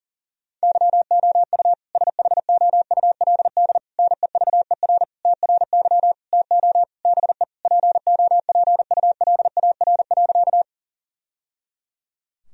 Morse code, heard as a keyed tone, about 40 wpm, 700 Hz. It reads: YOU SHOULD NEVER TRY TO BE POPULAR.
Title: Sentences from Top 1000 Words - Review Part 3 - 40wpm